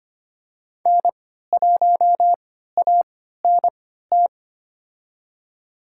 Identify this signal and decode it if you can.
Morse code, heard as a keyed tone, about 25 wpm, spaced as for 14 wpm, 700 Hz. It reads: N1ANT
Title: USA Call Signs - 5 Characters - 3x Character Spacing - 25wpm